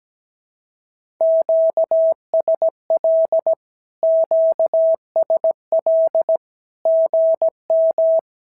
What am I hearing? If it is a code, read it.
Morse code, heard as a keyed tone, about 17 wpm, 650 Hz. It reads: QSL QSL GM
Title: SOTA QSO Practice 17wpm